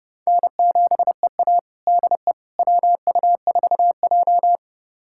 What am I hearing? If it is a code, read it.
Morse code, heard as a keyed tone, about 32 wpm, 700 Hz. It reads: N7EA DE WU4J
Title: USA Callsign DE Callsign - 4 Characters - 30wpm